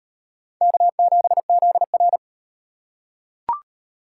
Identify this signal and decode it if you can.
Morse code, heard as a keyed tone, about 38 wpm, 700 Hz. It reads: K7ZR